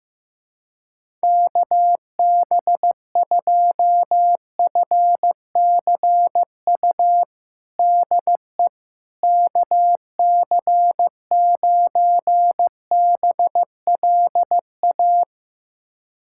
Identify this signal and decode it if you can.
Morse code, heard as a keyed tone, about 15 wpm, 700 Hz. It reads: KB2FCU DE KC9BLA
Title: USA Callsign DE Callsign 15wpm